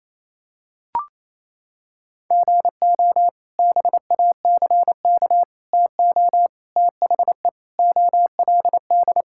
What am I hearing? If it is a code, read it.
Morse code, heard as a keyed tone, about 28 wpm, 700 Hz. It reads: GO BACK TO THE OLD
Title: Sentences from Top 1000 Words - Review Part 5 - 28wpm